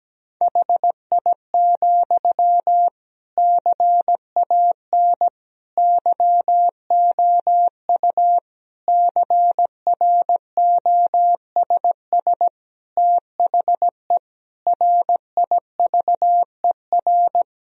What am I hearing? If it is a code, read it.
Morse code, heard as a keyed tone, about 17 wpm, 700 Hz. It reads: HI, CAN YOU CROSS THE RIVER
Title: Sentences from Top 1000 Words - Review Part 3 - 17wpm